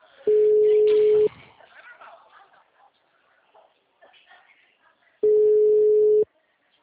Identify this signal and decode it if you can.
Ringback tone (Europe, 425 Hz), with speech.
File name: out- -2004- - - .7491